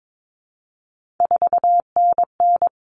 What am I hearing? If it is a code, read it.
Morse code, heard as a keyed tone, about 22 wpm, 700 Hz. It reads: 4NN
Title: Common Cut Numbers - 3-Digit 22wpm